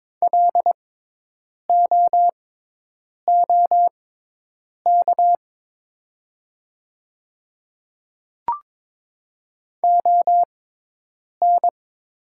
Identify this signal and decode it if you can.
Morse code, heard as a keyed tone, about 22 wpm, spaced as for 8 wpm, 690 Hz. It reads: LOOK ON
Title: 6x Character Spacing - Top 100 Words - 22wpm